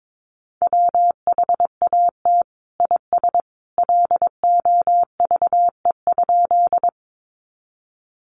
Morse code, heard as a keyed tone, about 22 wpm, 700 Hz. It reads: WHAT IS LOVE?
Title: Sentences from Top 400 Words 22wpm